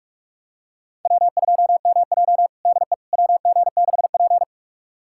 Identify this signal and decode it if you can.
Morse code, heard as a keyed tone, about 43 wpm, 700 Hz. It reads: W2MJ DE WG6P
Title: USA Callsign DE Callsign - 4 Characters - 45wpm